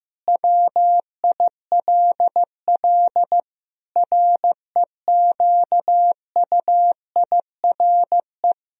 Morse code, heard as a keyed tone, about 15 wpm, 700 Hz. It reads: WILL REQUIRE
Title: Sentences from Top 1000 Words - Review Part 4 - 15wpm